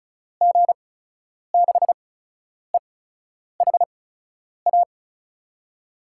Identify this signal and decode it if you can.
Morse code, heard as a keyed tone, about 34 wpm, spaced as for 10 wpm, 700 Hz. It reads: G6EHA
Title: USA Call Signs - 6 Characters - 8x Character Spacing - 35wpm